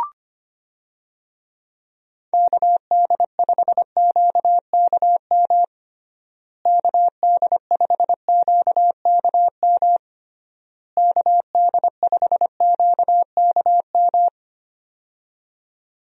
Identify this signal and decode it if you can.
Morse code, heard as a keyed tone, about 25 wpm, 700 Hz. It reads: KD5QKM KD5QKM KD5QKM
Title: USA Call Signs - 6 Characters - Repeated 3x - 25wpm